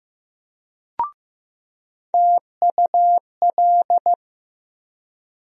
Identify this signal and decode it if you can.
Morse code, heard as a keyed tone, about 15 wpm, 700 Hz. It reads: TUL
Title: USA Call Sign Postfixes - 3 Letters 15wpm